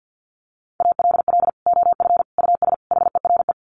Morse code, heard as a keyed tone, about 50 wpm, 700 Hz. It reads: ALL OF US HERE
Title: Sets of 5 Words 50wpm